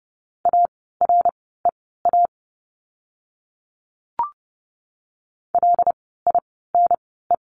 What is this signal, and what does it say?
Morse code, heard as a keyed tone, about 30 wpm, spaced as for 17 wpm, 700 Hz. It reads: AREA LINE
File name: course / 30:10wpm / 115 Lesson 19f - Letter w - Words 30wpm